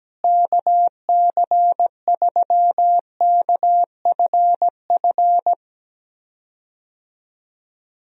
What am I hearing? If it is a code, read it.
Morse code, heard as a keyed tone, about 17 wpm, 700 Hz. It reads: KC3KFF
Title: USA Call Signs - 6 Characters - Repeated 2x - 17wpm